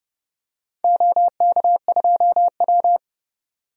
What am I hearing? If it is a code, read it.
Morse code, heard as a keyed tone, about 31 wpm, 700 Hz. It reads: OK2W
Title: DXCC-100 Call Signs - 4 Characters - 30wpm